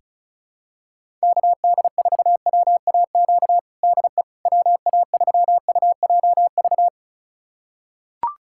Morse code, heard as a keyed tone, about 34 wpm, 700 Hz. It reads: KD4WAQ DE WA3UJV
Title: USA Callsign DE Callsign 35wpm